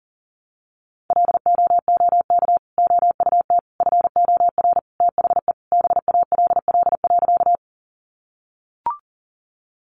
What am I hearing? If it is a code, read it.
Morse code, heard as a keyed tone, about 40 wpm, 700 Hz. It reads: LOOK OUT FOR THE BALL.
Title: Sentences from Top 900 Words 40wpm